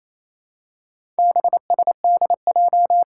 Morse code, heard as a keyed tone, about 28 wpm, 700 Hz. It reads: BSDJ
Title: Four Challenging Letter-Number Combinations 28wpm